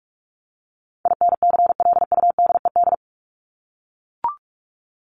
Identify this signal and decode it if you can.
Morse code, heard as a keyed tone, about 44 wpm, 700 Hz. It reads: INCLUDED
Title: ED Suffix - Top 100 Words with Base Words 45wpm